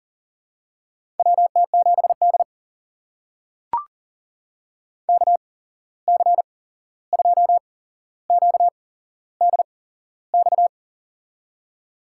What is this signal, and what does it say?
Morse code, heard as a keyed tone, about 40 wpm, spaced as for 15 wpm, 700 Hz. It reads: WT7D KC2QDX
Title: USA Call Signs - 8x Character Spacing - 40wpm